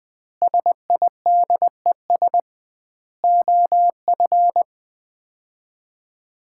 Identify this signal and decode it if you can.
Morse code, heard as a keyed tone, about 20 wpm, 700 Hz. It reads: SIDES OF
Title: 2x Word-Spacing - Sets of 4 Words - 20wpm